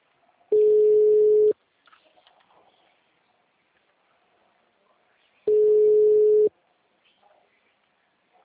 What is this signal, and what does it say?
Ringback tone (Europe, 425 Hz).
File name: out- -2008- - - .64865